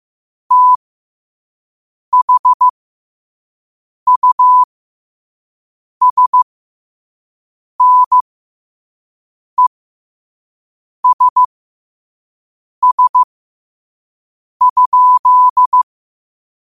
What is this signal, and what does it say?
Morse code, heard as a keyed tone, about 15 wpm, spaced as for 5 wpm, 1000 Hz. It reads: THUSNESS?
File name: qotdfiles / 5wpm